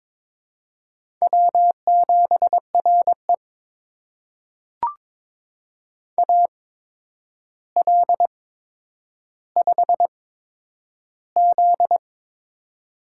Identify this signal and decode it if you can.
Morse code, heard as a keyed tone, about 22 wpm, spaced as for 9 wpm, 700 Hz. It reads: W7RE AL5Z
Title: USA Call Signs - 4 Characters - 8x Character Spacing - 22wpm